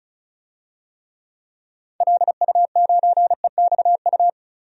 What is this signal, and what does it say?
Morse code, heard as a keyed tone, about 36 wpm, 700 Hz. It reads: LU9EXU
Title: DXCC-100 Call Signs - 6 Characters - Repeated 3x - 35wpm